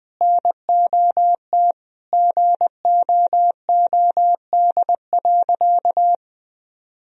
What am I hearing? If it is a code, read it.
Morse code, heard as a keyed tone, about 20 wpm, 700 Hz. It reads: NOT GOOD.
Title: Sentences from Top 1000 Words - Review Part 4 - 20wpm